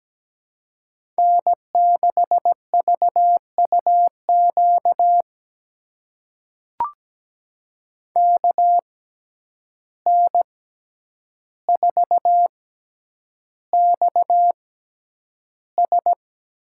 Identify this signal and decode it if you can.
Morse code, heard as a keyed tone, about 17 wpm, spaced as for 9 wpm, 700 Hz. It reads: N6VUQ KN4XS
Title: USA Call Signs - 5 Characters - 6x Character Spacing - 17wpm